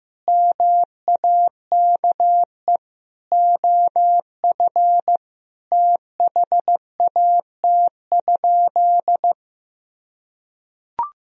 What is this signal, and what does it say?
Morse code, heard as a keyed tone, about 16 wpm, 700 Hz. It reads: MAKE OF THAT?
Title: Sentences from Top 100 Words and Abbreviations 15wpm